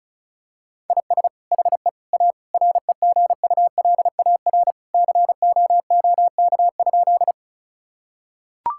Morse code, heard as a keyed tone, about 34 wpm, 700 Hz. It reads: IS HE A REGULAR COOK?
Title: Sentences from Top 1000 Words - Review Part 6 - 35wpm